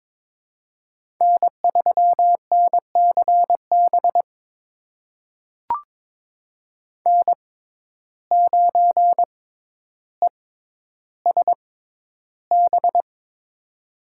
Morse code, heard as a keyed tone, about 22 wpm, spaced as for 11 wpm, 700 Hz. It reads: N3NCB N9ESB
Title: USA Call Signs - 5 Characters - 6x Character Spacing - 22wpm